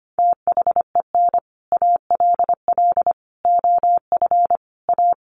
Morse code, heard as a keyed tone, about 25 wpm, 700 Hz. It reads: THEN ALL OF A
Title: Sets of 5 Words Encore 25wpm